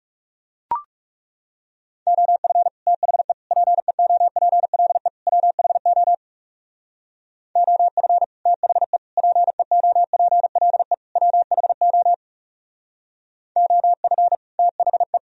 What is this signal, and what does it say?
Morse code, heard as a keyed tone, about 35 wpm, 700 Hz. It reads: OF THE PEOPLE WHO OF THE PEOPLE WHO OF THE
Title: Speed-Racing - Sets of 4 Words 30wpm